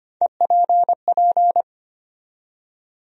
Morse code, heard as a keyed tone, about 25 wpm, 700 Hz. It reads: EPP
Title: USA Call Sign Postfixes - 3 Letters 25wpm